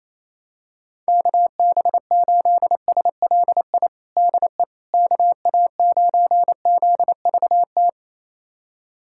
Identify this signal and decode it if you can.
Morse code, heard as a keyed tone, about 28 wpm, 700 Hz. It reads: KB8SLI DE KA9ZVT
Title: USA Callsign DE Callsign - 6 Characters - 28wpm